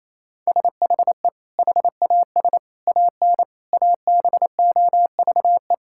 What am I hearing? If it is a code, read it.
Morse code, heard as a keyed tone, about 27 wpm, 700 Hz. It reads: SHE HAS AN ABOVE
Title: Sentences from Top 1000 Words - Review Part 6 - 28wpm